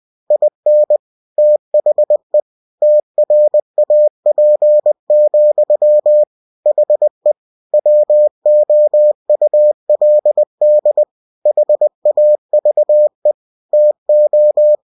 Morse code, heard as a keyed tone, about 20 wpm, 600 Hz. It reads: IN THE TRAP, HE WOULD HAVE TO
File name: cw-books / winnie-the-pooh / winnie-20wpm-page-0037